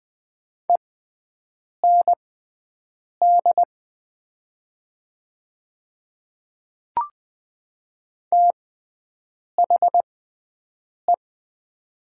Morse code, heard as a keyed tone, about 20 wpm, spaced as for 7 wpm, 700 Hz. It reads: END THE
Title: 6x Character Spacing - Top 200 Words - 20wpm